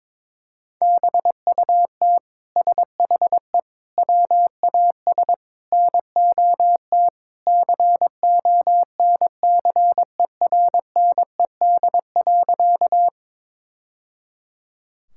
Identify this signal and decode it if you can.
Morse code, heard as a keyed tone, about 22 wpm, 700 Hz. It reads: BUT SHE WAS NOT CONCERNED.
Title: Sentences from Top 1000 Words - Review Part 6 - 22wpm